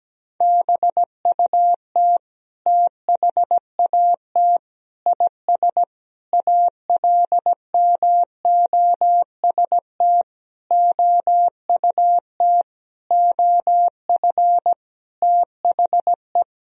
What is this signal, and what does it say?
Morse code, heard as a keyed tone, about 17 wpm, 700 Hz. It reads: BUT THAT IS ALMOST OUT OF THE
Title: Sentences from Top 1000 Words - Review Part 4 - 17wpm